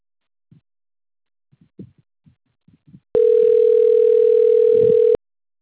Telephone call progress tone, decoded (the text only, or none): ringback tone